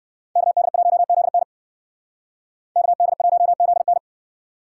Morse code, heard as a keyed tone, about 64 wpm, 700 Hz. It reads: KD1ZN KD1ZN